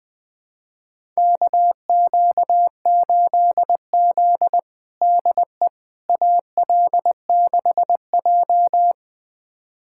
Morse code, heard as a keyed tone, about 21 wpm, 700 Hz. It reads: KQ8Z DE AL6J
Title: USA Callsign DE Callsign - 4 Characters - 20wpm